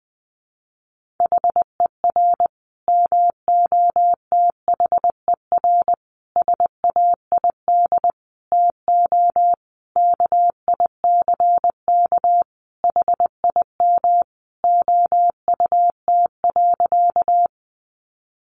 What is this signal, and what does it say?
Morse code, heard as a keyed tone, about 20 wpm, 700 Hz. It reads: HER MOTHER SAID TO KICK HIM OUT.
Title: Sentences from Top 1300 Words 20wpm